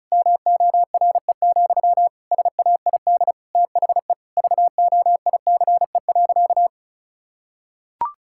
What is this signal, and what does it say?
Morse code, heard as a keyed tone, about 34 wpm, 700 Hz. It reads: MORE, SAID THE VOICE.
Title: Sentences from Top 1600 Words 35wpm